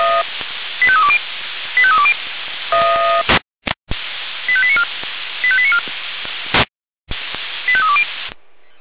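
Five-tone selective call (ZVEI), heard as quick runs of five tones, tones of about 70 ms. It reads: 85320 85320 85894 85894 85320
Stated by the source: ZVEI five-tone call received by SDR.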